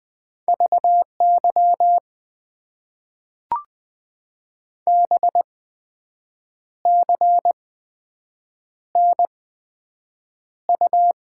Morse code, heard as a keyed tone, about 20 wpm, spaced as for 7 wpm, 700 Hz. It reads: VY BCNU
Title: Top 90 QSO Elements - 8x Character Spacing - 20wpm